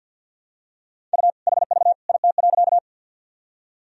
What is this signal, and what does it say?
Morse code, heard as a keyed tone, about 53 wpm, 700 Hz. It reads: U HV IT.
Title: Sentences from Top 100 Words and Abbreviations 50wpm